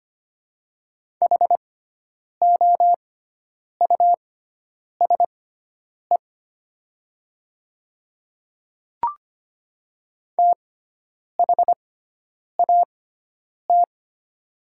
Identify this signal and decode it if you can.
Morse code, heard as a keyed tone, about 25 wpm, spaced as for 9 wpm, 700 Hz. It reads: HOUSE THAT
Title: 6x Character Spacing - Top 200 Words - 25wpm